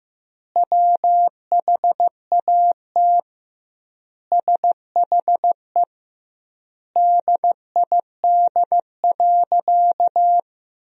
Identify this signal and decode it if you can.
Morse code, heard as a keyed tone, about 15 wpm, 700 Hz. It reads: WHAT SHE DID.
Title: 2x Word-Spacing - Sentences from Top 100 Words - 15wpm